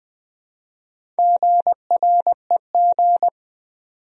Morse code, heard as a keyed tone, about 20 wpm, 700 Hz. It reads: GREG
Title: USA Common Names - Max 5 Letters - 20wpm